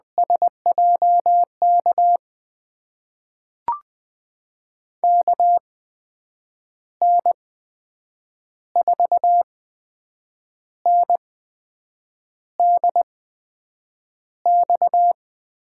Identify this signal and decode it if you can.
Morse code, heard as a keyed tone, about 20 wpm, spaced as for 7 wpm, 700 Hz. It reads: SJK KN4NDX
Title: USA Call Signs - 6 Characters - 8x Character Spacing - 20wpm